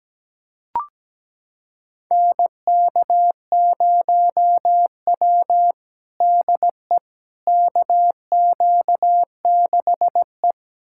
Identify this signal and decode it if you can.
Morse code, heard as a keyed tone, about 17 wpm, 700 Hz. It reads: NK0W DE KQ6E
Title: USA Callsign DE Callsign - 4 Characters - 17wpm